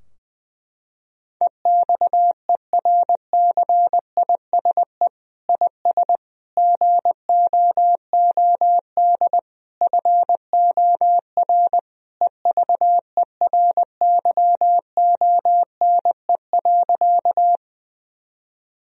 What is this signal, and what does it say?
Morse code, heard as a keyed tone, about 21 wpm, 700 Hz. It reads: EXERCISE IS GOOD FOR EVERYONE.
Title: Sentences from Top 1000 Words - Review Part 4 - 20wpm